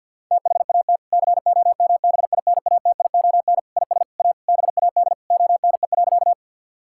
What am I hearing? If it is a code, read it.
Morse code, heard as a keyed tone, about 50 wpm, 700 Hz. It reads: THAT COMBINATION IS A BAD ONE.
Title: Sentences from Top 1000 Words - Review Part 6 - 50wpm